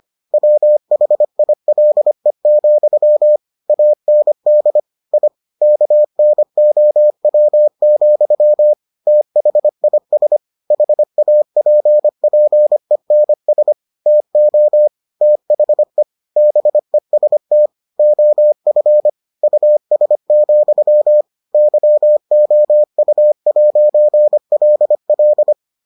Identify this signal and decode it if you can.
Morse code, heard as a keyed tone, about 25 wpm, 600 Hz. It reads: WHILE, AND I KNOW, THIS HAPPENS TO THE BEST OF US, YOU'LL